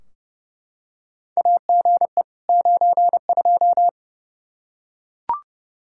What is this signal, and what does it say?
Morse code, heard as a keyed tone, about 29 wpm, 700 Hz. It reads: AGE 92
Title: Practice QSO - Exchange 3 - More Info - 30wpm